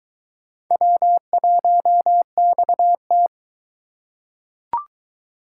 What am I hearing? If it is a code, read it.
Morse code, heard as a keyed tone, about 22 wpm, 700 Hz. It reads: W1XT